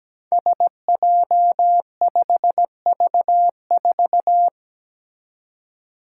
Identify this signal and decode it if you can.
Morse code, heard as a keyed tone, about 17 wpm, 700 Hz. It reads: SJ5V4
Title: Five Challenging Letter-Number Combinations 17wpm